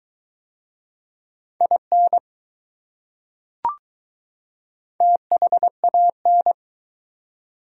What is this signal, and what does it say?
Morse code, heard as a keyed tone, about 20 wpm, 700 Hz. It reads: IN THAN